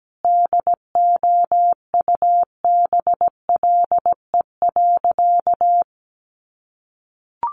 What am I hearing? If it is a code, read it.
Morse code, heard as a keyed tone, about 17 wpm, 700 Hz. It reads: DOUBLE.
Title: Sentences from Top 1400 Words 17wpm